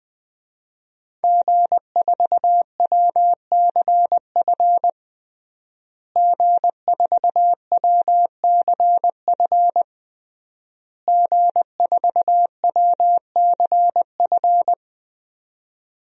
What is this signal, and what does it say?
Morse code, heard as a keyed tone, about 20 wpm, 700 Hz. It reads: G4WCF G4WCF G4WCF
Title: DXCC-100 Call Signs - 5 Characters - Repeated 3x - 20wpm